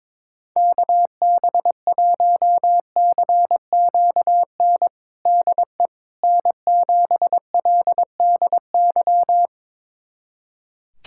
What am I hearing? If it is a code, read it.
Morse code, heard as a keyed tone, about 22 wpm, 700 Hz. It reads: KB1CQN DE N7LDY
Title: USA Callsign DE Callsign 22wpm